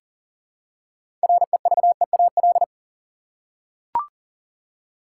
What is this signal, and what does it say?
Morse code, heard as a keyed tone, about 37 wpm, 700 Hz. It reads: REVEAL
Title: Six Letter Words - 40wpm